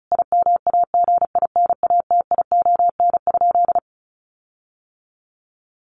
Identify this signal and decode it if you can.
Morse code, heard as a keyed tone, about 34 wpm, 700 Hz. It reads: IMAGINATION?
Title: Sentences from Top 1300 Words 35wpm